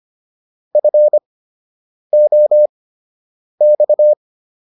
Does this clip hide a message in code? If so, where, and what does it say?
Yes: Morse code from 0.8 to 4.1 s, heard as a keyed tone, about 25 wpm, spaced as for 8 wpm, 600 Hz. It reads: FOX